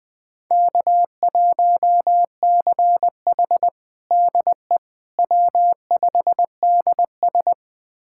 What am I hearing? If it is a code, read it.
Morse code, heard as a keyed tone, about 21 wpm, 700 Hz. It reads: K1CH DE W5DS
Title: USA Callsign DE Callsign - 4 Characters - 20wpm